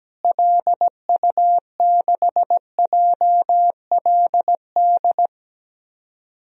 Morse code, heard as a keyed tone, about 17 wpm, 700 Hz. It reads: LU6JLD
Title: DXCC-100 Call Signs - 6 Characters - Repeated 3x - 17wpm